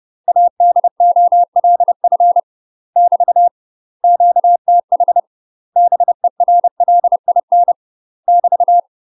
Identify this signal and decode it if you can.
Morse code, heard as a keyed tone, about 30 wpm, 700 Hz. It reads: ADOLF <BT> QTH BERLIN <BT>
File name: UK MASTER QSOs TRAINING / 30WPM / UK MASTER QSOs 0039